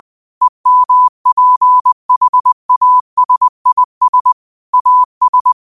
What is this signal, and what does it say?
Morse code, heard as a keyed tone, about 20 wpm, 1000 Hz. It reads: EMPHASIS AS
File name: qotdfiles / 20wpm